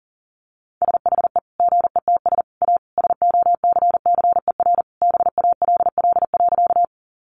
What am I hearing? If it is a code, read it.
Morse code, heard as a keyed tone, about 40 wpm, 700 Hz. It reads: SHE GETS A SOCCER BALL.
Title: Easy Phrases 40wpm